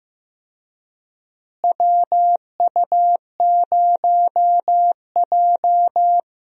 Morse code, heard as a keyed tone, about 15 wpm, 700 Hz. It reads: WU0J
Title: USA Call Signs - 4 Characters - Repeated 3x - 15wpm